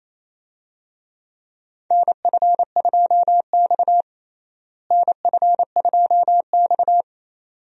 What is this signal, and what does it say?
Morse code, heard as a keyed tone, about 28 wpm, 700 Hz. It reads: NF2X NF2X
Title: USA Call Signs - 4 Characters - Repeated 2x - 28wpm